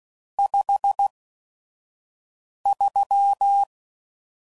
Morse code, heard as a keyed tone, about 16 wpm, spaced as for 5 wpm, 800 Hz. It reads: 53
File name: CW Code Groups @5WPM / Weather Sets / Weather 5 1